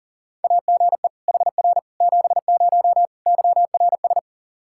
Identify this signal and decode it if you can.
Morse code, heard as a keyed tone, about 40 wpm, 700 Hz. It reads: AGE HR 70 YRS
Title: Practice QSO - Exchange 3 - More Info - 40wpm